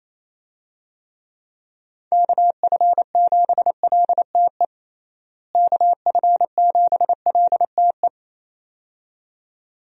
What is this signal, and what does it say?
Morse code, heard as a keyed tone, about 28 wpm, 700 Hz. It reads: KF7LTE KF7LTE
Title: USA Call Signs - 6 Characters - Repeated 2x - 28wpm